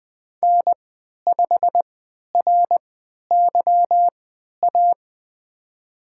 Morse code, heard as a keyed tone, about 20 wpm, spaced as for 11 wpm, 700 Hz. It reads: N5RYA
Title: USA Call Signs - 3x Character Spacing - 20wpm